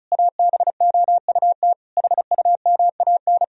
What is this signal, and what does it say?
Morse code, heard as a keyed tone, about 36 wpm, 700 Hz. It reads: ABOUT HUMAN
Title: Sentences from Top 1000 Words - Review Part 4 - 35wpm